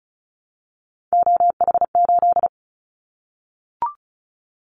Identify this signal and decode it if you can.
Morse code, heard as a keyed tone, about 34 wpm, 700 Hz. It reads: OH8
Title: Callsign Prefixes for Top 50 DXCC - 35wpm